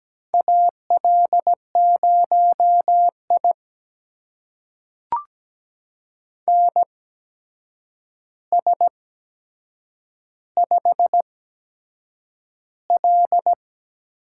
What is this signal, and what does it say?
Morse code, heard as a keyed tone, about 17 wpm, spaced as for 7 wpm, 700 Hz. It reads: AL0I NS5L
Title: USA Call Signs - 4 Characters - 8x Character Spacing - 17wpm